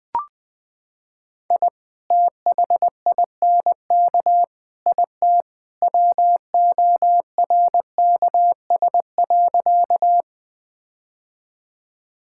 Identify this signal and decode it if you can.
Morse code, heard as a keyed tone, about 20 wpm, 700 Hz. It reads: I THINK IT WORKS.
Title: Sentences with 2-Word Phrases - 20wpm